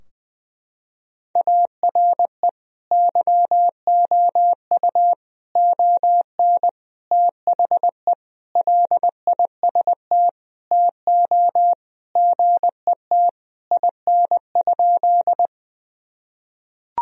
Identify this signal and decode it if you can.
Morse code, heard as a keyed tone, about 20 wpm, 700 Hz. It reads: ARE YOU ON THE LIST TO GET IN?
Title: Sentences from Top 1000 Words - Review Part 2 - 20wpm